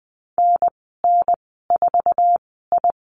Morse code, heard as a keyed tone, about 19 wpm, spaced as for 14 wpm, 700 Hz. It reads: NN4I
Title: 048 Lesson 9g - Number 4 - Call Signs 20wpm